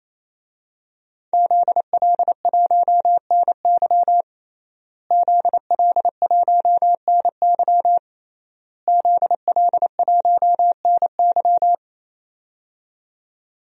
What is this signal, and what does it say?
Morse code, heard as a keyed tone, about 27 wpm, 700 Hz. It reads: ZL1NY ZL1NY ZL1NY
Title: DXCC-100 Call Signs - 5 Characters - Repeated 3x - 28wpm